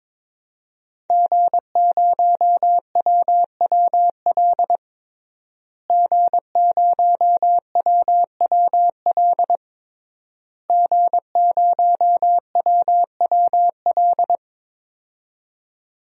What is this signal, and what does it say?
Morse code, heard as a keyed tone, about 22 wpm, 700 Hz. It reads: G0WWL G0WWL G0WWL
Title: DXCC-100 Call Signs - 5 Characters - Repeated 3x - 22wpm